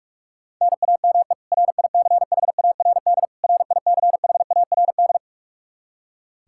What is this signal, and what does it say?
Morse code, heard as a keyed tone, about 46 wpm, 700 Hz. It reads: NAME RICHARD RICHARD
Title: Practice QSO - Exchange 1 - Beginning - 45wpm